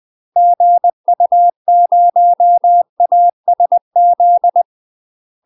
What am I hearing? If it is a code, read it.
Morse code, heard as a keyed tone, about 20 wpm, 700 Hz. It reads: GU0ASZ
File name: UK MASTER QSOs TRAINING / 20WPM / UK MASTER QSOs 0067